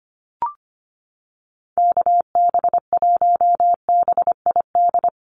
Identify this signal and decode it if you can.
Morse code, heard as a keyed tone, about 25 wpm, 700 Hz. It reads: KB1BID
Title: USA Call Signs Repeated 2x 25wpm